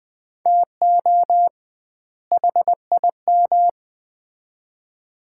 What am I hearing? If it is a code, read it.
Morse code, heard as a keyed tone, about 20 wpm, 700 Hz. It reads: TO HIM
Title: 2x Word-Spacing - Sets of 2 Words - 20wpm